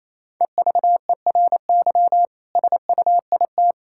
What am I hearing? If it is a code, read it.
Morse code, heard as a keyed tone, about 28 wpm, 700 Hz. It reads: EVERY SUIT
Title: Sentences from Top 1600 Words 28wpm